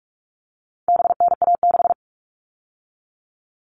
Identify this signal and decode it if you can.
Morse code, heard as a keyed tone, about 44 wpm, 700 Hz. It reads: BNA6
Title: All Cut Numbers - 4-Digit 45wpm